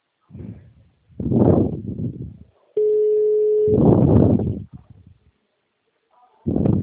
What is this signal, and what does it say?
Ringback tone (Europe, 425 Hz).
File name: out- -2018- - - .52105